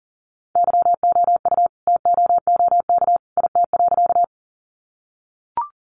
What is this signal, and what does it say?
Morse code, heard as a keyed tone, about 38 wpm, 700 Hz. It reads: YOU TOOK IT.
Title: Sentences from Top 300 Words 40wpm